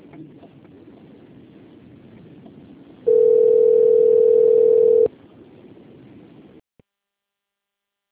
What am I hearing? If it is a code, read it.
Ringback tone (North America), 440 + 480 Hz.